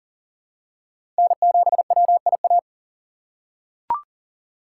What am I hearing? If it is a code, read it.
Morse code, heard as a keyed tone, about 40 wpm, 700 Hz. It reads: N7WIA
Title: USA Call Signs - 5 Characters - 6x Character Spacing - 40wpm